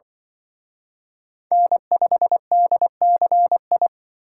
Morse code, heard as a keyed tone, about 25 wpm, 700 Hz. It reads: N5DCI
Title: Speed Racing - USA 1x3 Call Signs 20wpm